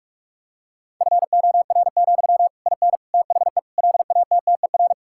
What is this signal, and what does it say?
Morse code, heard as a keyed tone, about 46 wpm, 700 Hz. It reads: FOR, IN THE LATTER
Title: Sentences from Top 1600 Words 45wpm